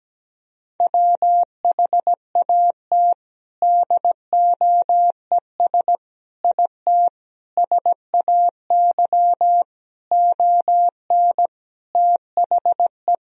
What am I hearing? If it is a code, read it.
Morse code, heard as a keyed tone, about 17 wpm, 700 Hz. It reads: WHAT DOES IT SAY ON THE
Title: Sentences from Top 1000 Words - Review Part 2 - 17wpm